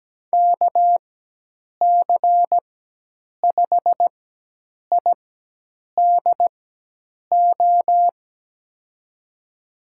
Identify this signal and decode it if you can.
Morse code, heard as a keyed tone, about 17 wpm, spaced as for 8 wpm, 700 Hz. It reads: KC5IDO
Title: USA Call Signs - 6 Characters - 4x Character Spacing - 17wpm